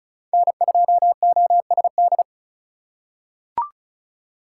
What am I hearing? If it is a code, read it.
Morse code, heard as a keyed tone, about 36 wpm, 700 Hz. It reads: N2OSD
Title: USA Call Signs - 5 Characters - 8x Character Spacing - 35wpm